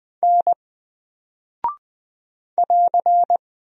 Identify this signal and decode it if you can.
Morse code, heard as a keyed tone, about 20 wpm, 700 Hz. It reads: N <AR>
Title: Single Character (Letter:Number:Punctuation:Prosign) Alt 20wpm